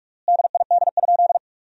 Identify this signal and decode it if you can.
Morse code, heard as a keyed tone, about 46 wpm, 700 Hz. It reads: DID?
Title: 3x Word-Spacing - Sentences from Top 100 Words - 45wpm